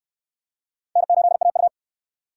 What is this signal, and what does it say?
Morse code, heard as a keyed tone, about 68 wpm, 700 Hz. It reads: N8AU